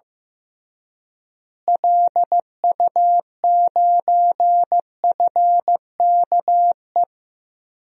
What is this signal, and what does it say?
Morse code, heard as a keyed tone, about 15 wpm, 700 Hz. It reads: LU9FKE
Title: DXCC-100 Call Signs - 6 Characters - Repeated 3x - 15wpm